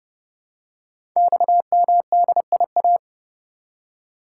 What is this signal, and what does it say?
Morse code, heard as a keyed tone, about 30 wpm, 700 Hz. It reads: XMDIA
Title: Code Groups - 5 Letters - 30wpm